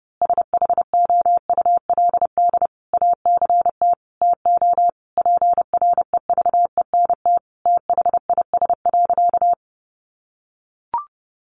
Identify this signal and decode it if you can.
Morse code, heard as a keyed tone, about 30 wpm, 700 Hz. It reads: SHOULD ACT TO PREVENT THIS.
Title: Sentences from Top 1100 Words 30wpm